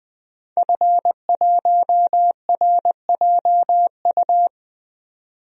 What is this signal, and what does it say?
Morse code, heard as a keyed tone, about 20 wpm, 700 Hz. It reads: F1RJU
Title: DXCC-100 Call Signs - 5 Characters - 20wpm